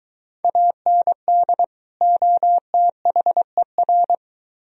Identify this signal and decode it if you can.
Morse code, heard as a keyed tone, about 23 wpm, 700 Hz. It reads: AND OTHER